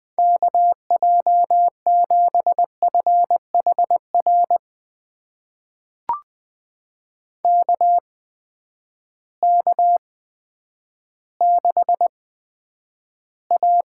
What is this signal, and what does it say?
Morse code, heard as a keyed tone, about 20 wpm, spaced as for 10 wpm, 700 Hz. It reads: KJ7FHR KK6A
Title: USA Call Signs - 6 Characters - 8x Character Spacing - 20wpm